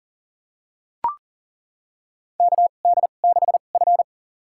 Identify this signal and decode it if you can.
Morse code, heard as a keyed tone, about 40 wpm, spaced as for 29 wpm, 700 Hz. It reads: KD6F
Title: USA Call Signs - 4 Characters - 2x Character Spacing - 40wpm